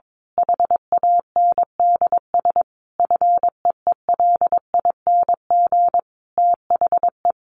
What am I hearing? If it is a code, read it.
Morse code, heard as a keyed tone, about 22 wpm, 700 Hz. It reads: HANDS FEELING THE
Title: A Sound Of Thunder 22wpm